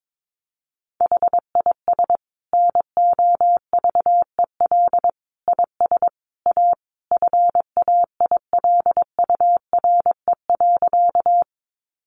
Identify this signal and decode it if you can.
Morse code, heard as a keyed tone, about 22 wpm, 700 Hz. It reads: HIS NOVEL IS A FAILURE.
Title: Sentences from Top 1300 Words 22wpm